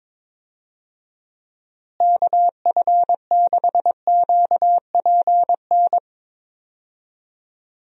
Morse code, heard as a keyed tone, about 22 wpm, 700 Hz. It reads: KF6QPN
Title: USA Call Signs - 6 Characters - Repeated 2x - 22wpm